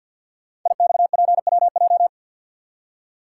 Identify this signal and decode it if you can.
Morse code, heard as a keyed tone, about 53 wpm, 700 Hz. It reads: IXPWJ
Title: Code Groups - 5 Letters - 50wpm